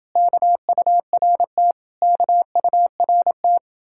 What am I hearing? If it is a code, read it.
Morse code, heard as a keyed tone, about 30 wpm, 700 Hz. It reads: KURT KURT